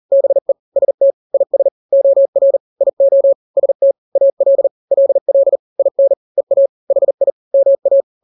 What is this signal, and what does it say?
Morse code, heard as a keyed tone, about 40 wpm, 550 Hz. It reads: BE ST IS OR IO ST AL LL IN EA HI MA
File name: n-grams / 2-grams 40 35 WPM